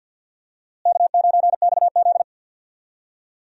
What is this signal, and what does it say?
Morse code, heard as a keyed tone, about 52 wpm, 700 Hz. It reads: K9XZ
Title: USA Call Signs - 4 Characters - Repeated 3x - 50wpm